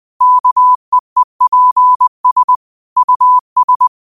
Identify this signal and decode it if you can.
Morse code, heard as a keyed tone, about 20 wpm, 1000 Hz. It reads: KEEPS US